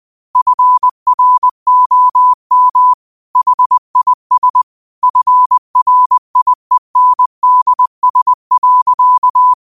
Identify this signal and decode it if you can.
Morse code, heard as a keyed tone, about 20 wpm, 1000 Hz. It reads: FROM HIS FRIENDS.
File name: qotdfiles / 20wpm